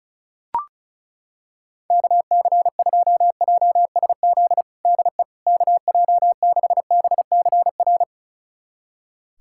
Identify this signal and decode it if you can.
Morse code, heard as a keyed tone, about 36 wpm, 700 Hz. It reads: KC2JSZ DE KJ6BCR
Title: USA Callsign DE Callsign - 6 Characters - 35wpm